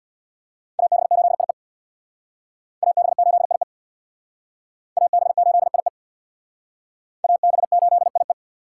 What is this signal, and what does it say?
Morse code, heard as a keyed tone, about 61 wpm, 700 Hz. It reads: AB8IE AB8IE AB8IE AB8IE
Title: Speed Racing - USA 2x2 Call Signs 50wpm